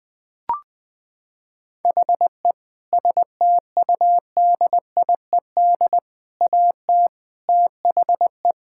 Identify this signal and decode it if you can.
Morse code, heard as a keyed tone, about 20 wpm, 700 Hz. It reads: HE STUDIED AT THE
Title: Sentences with 4-Word Phrases - 20wpm